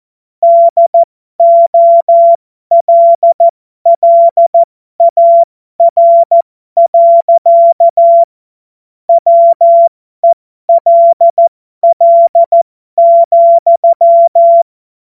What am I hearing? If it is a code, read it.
Morse code, heard as a keyed tone, about 14 wpm, spaced as for 12 wpm, 680 Hz. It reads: DOLLAR. WELL,